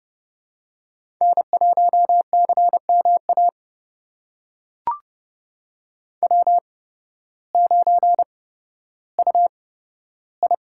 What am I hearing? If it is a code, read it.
Morse code, heard as a keyed tone, about 30 wpm, spaced as for 14 wpm, 700 Hz. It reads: N1CMA W9UI
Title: USA Call Signs - 8x Character Spacing - 30wpm